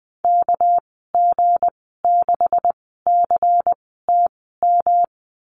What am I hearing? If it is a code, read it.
Morse code, heard as a keyed tone, about 20 wpm, spaced as for 15 wpm, 700 Hz. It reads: KG6CTM
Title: 231 Lesson 37g - Letter x - Call Signs 20wpm